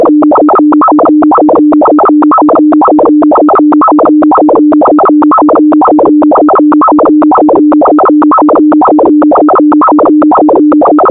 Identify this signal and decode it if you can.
Two-tone teleprinter signal, about 450 Hz shift.
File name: audio notizie / radia s24 n297 radiohelsinki raggam dataradio pt4